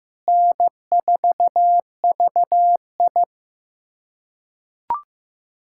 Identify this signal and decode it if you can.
Morse code, heard as a keyed tone, about 15 wpm, 700 Hz. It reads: N4VI
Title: USA Callsign DE Callsign - 4 Characters - 15wpm